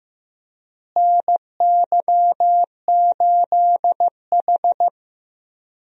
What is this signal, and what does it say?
Morse code, heard as a keyed tone, about 16 wpm, 700 Hz. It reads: NY8H
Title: USA Call Signs - 4 Characters - Repeated 3x - 15wpm